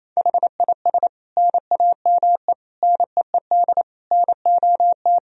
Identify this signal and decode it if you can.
Morse code, heard as a keyed tone, about 27 wpm, 700 Hz. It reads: HIS NAME NEED NOT
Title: Sentences from Top 1000 Words - Review Part 2 - 28wpm